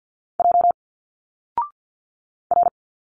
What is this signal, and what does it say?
Morse code, heard as a keyed tone, about 48 wpm, 700 Hz. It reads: J R
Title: Single Letters 50wpm